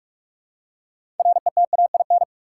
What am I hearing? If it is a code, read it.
Morse code, heard as a keyed tone, about 47 wpm, 700 Hz. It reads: RETAIN
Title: Six Letter Words - 45wpm